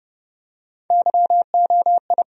Morse code, heard as a keyed tone, about 30 wpm, 700 Hz. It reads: YOI
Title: USA Call Sign Postfixes - 3 Letters 30wpm